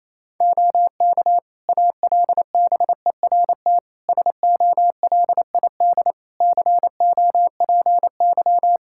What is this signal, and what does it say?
Morse code, heard as a keyed tone, about 28 wpm, 700 Hz. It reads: OK ALBERT SOLID COPY
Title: Practice QSO - Exchange 3 - More Info - 28wpm